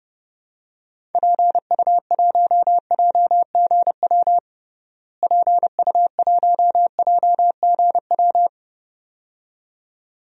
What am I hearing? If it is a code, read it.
Morse code, heard as a keyed tone, about 30 wpm, 700 Hz. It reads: PU1JGW PU1JGW
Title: DXCC-100 Call Signs - 6 Characters - Repeated 2x - 30wpm